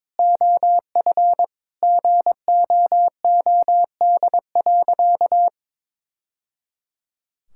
Morse code, heard as a keyed tone, about 21 wpm, 700 Hz. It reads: OF GOOD.
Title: Sentences from Top 1000 Words - Review Part 6 - 22wpm